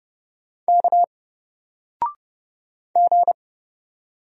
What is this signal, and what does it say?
Morse code, heard as a keyed tone, about 30 wpm, 700 Hz. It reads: K G
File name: Single-Letter-Number-30wpm